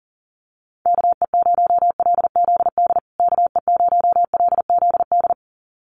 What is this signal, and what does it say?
Morse code, heard as a keyed tone, about 40 wpm, 700 Hz. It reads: KE0LZD KE0LZD
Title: USA Call Signs Repeated 2x 40wpm